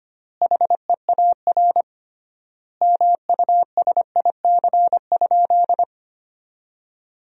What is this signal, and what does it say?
Morse code, heard as a keyed tone, about 25 wpm, 700 Hz. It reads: HEAR MUSIC?
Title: 3x Word-Spacing - Sentences from Top 300 Words - 25wpm